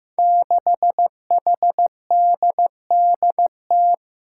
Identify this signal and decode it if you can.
Morse code, heard as a keyed tone, about 15 wpm, 700 Hz. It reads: 6HDDT
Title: Five Challenging Letter-Number Combinations 15wpm